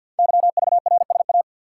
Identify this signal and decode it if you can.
Morse code, heard as a keyed tone, about 48 wpm, 690 Hz. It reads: YVRSA
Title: Code Groups - 5 Letters - 50wpm